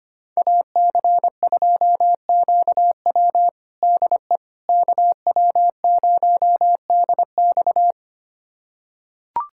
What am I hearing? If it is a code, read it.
Morse code, heard as a keyed tone, about 25 wpm, 700 Hz. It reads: AC2QW DE KW0DX
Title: USA Callsign DE Callsign - 5 Characters - 25wpm